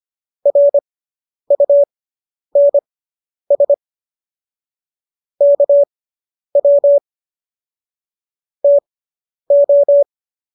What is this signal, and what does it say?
Morse code, heard as a keyed tone, about 25 wpm, spaced as for 10 wpm, 570 Hz. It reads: RUNS KW TO